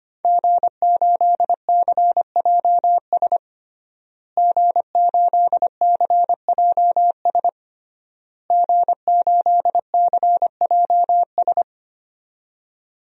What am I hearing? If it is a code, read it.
Morse code, heard as a keyed tone, about 25 wpm, 700 Hz. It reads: G8CJS G8CJS G8CJS
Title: DXCC-100 Call Signs - 5 Characters - Repeated 3x - 25wpm